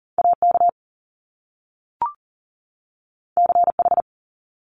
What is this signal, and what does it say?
Morse code, heard as a keyed tone, about 41 wpm, 700 Hz. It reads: AK CH
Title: Top Two Letter Combinations in Words 40wpm